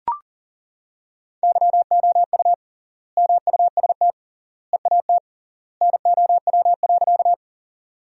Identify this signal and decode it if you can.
Morse code, heard as a keyed tone, about 40 wpm, 700 Hz. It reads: YOU MUST EAT NOW.
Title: 3x Word-Spacing - Sentences from Top 300 Words - 40wpm